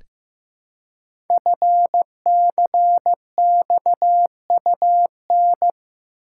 Morse code, heard as a keyed tone, about 15 wpm, 700 Hz. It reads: FCXUN
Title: Code Groups - 5 Letters - 15wpm